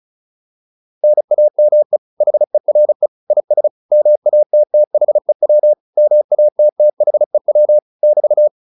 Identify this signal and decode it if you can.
Morse code, heard as a keyed tone, about 34 wpm, 600 Hz. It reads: NAME HERE IS MATTHEW MATTHEW <BT>